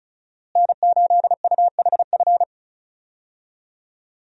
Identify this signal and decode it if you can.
Morse code, heard as a keyed tone, about 36 wpm, 700 Hz. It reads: N8UHF
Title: USA Call Signs - 5 Characters - Repeated 2x - 35wpm